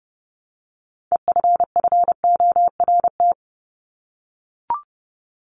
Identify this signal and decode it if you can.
Morse code, heard as a keyed tone, about 30 wpm, 700 Hz. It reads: EFFORT
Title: Top 1001-1250 Words 30wpm